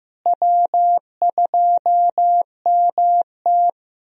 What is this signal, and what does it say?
Morse code, heard as a keyed tone, about 15 wpm, 700 Hz. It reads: W2MT
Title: USA Call Signs - 4 Characters - Repeated 2x - 15wpm